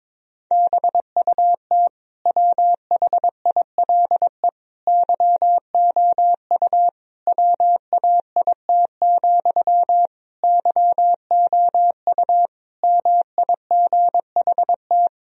Morse code, heard as a keyed tone, about 22 wpm, 700 Hz. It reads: BUT WHILE YOU WAIT, YOU MIGHT
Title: Sentences from Top 1000 Words - Review Part 1 - 22wpm